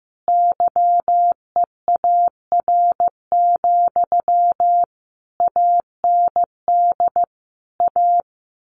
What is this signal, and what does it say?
Morse code, heard as a keyed tone, about 15 wpm, 700 Hz. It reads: YEAR, AND A
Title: A Sound Of Thunder 15wpm